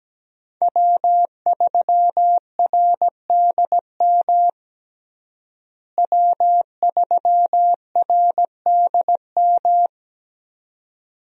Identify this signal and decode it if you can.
Morse code, heard as a keyed tone, about 17 wpm, 700 Hz. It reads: W3RDM W3RDM
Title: USA Call Signs - 5 Characters - Repeated 2x - 17wpm